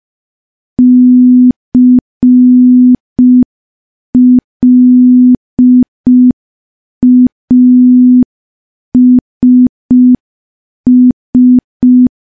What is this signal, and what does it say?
Morse code, heard as a keyed tone, about 5 wpm, 250 Hz. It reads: CLASS
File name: morse / morse-training-05WPM